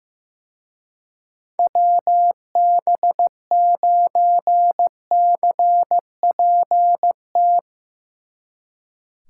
Morse code, heard as a keyed tone, about 17 wpm, 700 Hz. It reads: WB9CPT
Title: Speed Racing - USA 2x3 Call Signs 15wpm